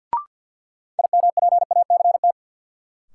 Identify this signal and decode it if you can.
Morse code, heard as a keyed tone, about 49 wpm, 700 Hz. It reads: IMPACT
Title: Sentences from Top 1000 Words - Review Part 6 - New Word Drill - 50wpm